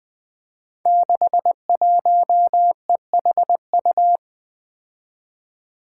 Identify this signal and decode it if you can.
Morse code, heard as a keyed tone, about 20 wpm, 700 Hz. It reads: 61EHU
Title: Five Challenging Letter-Number Combinations 20wpm